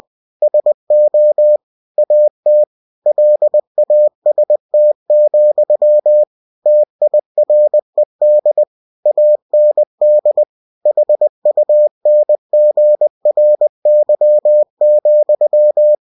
Morse code, heard as a keyed tone, about 20 wpm, 600 Hz. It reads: SO AT LAST, TIRED AND HUNGRY,
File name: cw-books / winnie-the-pooh / winnie-20wpm-page-0028